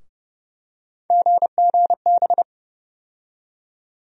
Morse code, heard as a keyed tone, about 29 wpm, 700 Hz. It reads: GGB
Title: USA Call Sign Postfixes - 3 Letters 30wpm